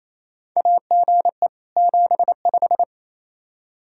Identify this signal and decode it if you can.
Morse code, heard as a keyed tone, about 28 wpm, 700 Hz. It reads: AGE 75
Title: Practice QSO - Exchange 3 - More Info - 28wpm